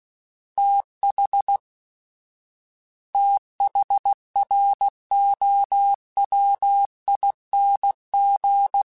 Morse code, heard as a keyed tone, about 16 wpm, 800 Hz. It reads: TH THROWING
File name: TH / TH mp3s / TH Prefixes 1x1 @15